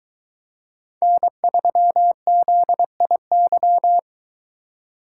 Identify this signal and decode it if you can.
Morse code, heard as a keyed tone, about 23 wpm, 700 Hz. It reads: N3ZIY